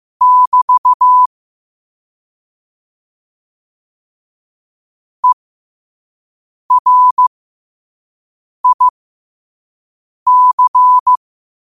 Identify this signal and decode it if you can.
Morse code, heard as a keyed tone, about 15 wpm, spaced as for 5 wpm, 1000 Hz. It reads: <BT> ERIC
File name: qotdfiles / 5wpm